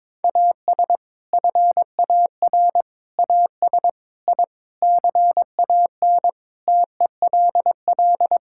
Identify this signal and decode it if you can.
Morse code, heard as a keyed tone, about 22 wpm, 700 Hz. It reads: AS FAR AS I CAN TELL
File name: practice / 22wpm / Sets of 7 Words - 22wpm